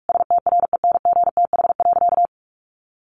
Morse code, heard as a keyed tone, about 44 wpm, 700 Hz. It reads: STRENGTH.
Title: Sentences from Top 1300 Words 45wpm